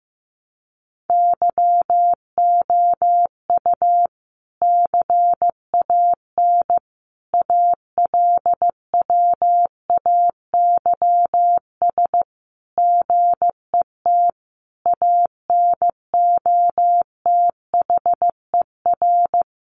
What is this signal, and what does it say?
Morse code, heard as a keyed tone, about 14 wpm, 700 Hz. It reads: YOU CAN ALWAYS GET ANOTHER
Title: Sentences from Top 1200 Words 15wpm